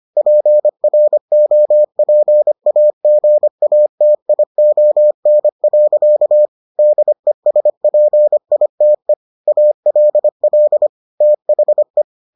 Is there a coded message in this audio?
Yes: Morse code from 0.2 to 12.0 s, heard as a keyed tone, about 25 wpm, 600 Hz. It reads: PROPAGATION. DESPITE ALL THE